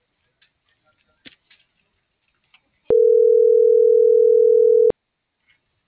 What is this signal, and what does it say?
Ringback tone (North America), 440 + 480 Hz.